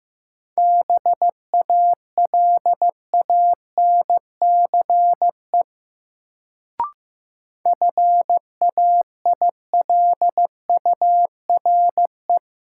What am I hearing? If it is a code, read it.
Morse code, heard as a keyed tone, about 16 wpm, 700 Hz. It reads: BALANCE FAILURE
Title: Sentences from Top 900 Words - New Word Drill - 15wpm